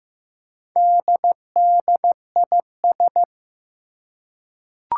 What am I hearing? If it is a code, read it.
Morse code, heard as a keyed tone, about 15 wpm, 700 Hz. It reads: DDIS
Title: Four Challenging Letter-Number Combinations 15wpm